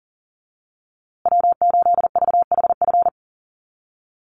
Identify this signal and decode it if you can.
Morse code, heard as a keyed tone, about 40 wpm, 700 Hz. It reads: W8VHF
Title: USA Call Signs Repeated 2x 40wpm